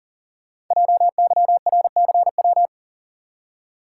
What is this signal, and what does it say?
Morse code, heard as a keyed tone, about 41 wpm, 700 Hz. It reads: JYRCW
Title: Code Groups - 5 Letter:Numbers - 40wpm